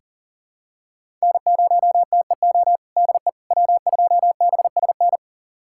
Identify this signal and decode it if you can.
Morse code, heard as a keyed tone, about 40 wpm, 700 Hz. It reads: N0TEO DE W2BSN
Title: USA Callsign DE Callsign - 5 Characters - 40wpm